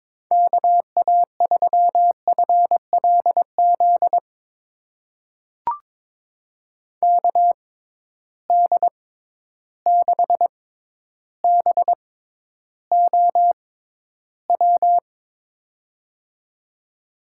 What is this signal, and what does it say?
Morse code, heard as a keyed tone, about 22 wpm, spaced as for 11 wpm, 700 Hz. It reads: KA3FLZ KD6BOW
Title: USA Call Signs - 6 Characters - 6x Character Spacing - 22wpm